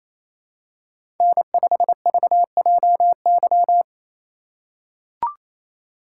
Morse code, heard as a keyed tone, about 29 wpm, 700 Hz. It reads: N5VJY
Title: USA Call Signs - 3x Character Spacing - 28wpm